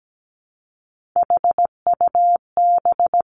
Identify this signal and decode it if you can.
Morse code, heard as a keyed tone, about 17 wpm, 700 Hz. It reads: HUB
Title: Three Challenging Letter-Number Combinations 17wpm